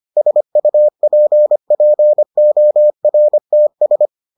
Morse code, heard as a keyed tone, about 25 wpm, 600 Hz. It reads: SUPPORTS